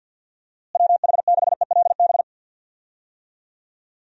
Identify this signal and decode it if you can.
Morse code, heard as a keyed tone, about 52 wpm, 700 Hz. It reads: WH6ELB
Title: USA Call Signs - 6 Characters - Repeated 3x - 50wpm